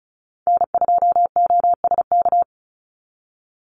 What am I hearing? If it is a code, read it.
Morse code, heard as a keyed tone, about 34 wpm, 700 Hz. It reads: N2OSK
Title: USA Call Signs 35wpm